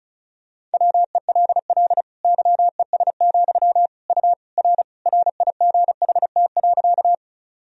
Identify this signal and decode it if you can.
Morse code, heard as a keyed tone, about 36 wpm, 700 Hz. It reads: WELL YES, U R RIGHT.
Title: Sentences from Top 500 Words and Abbreviations 35wpm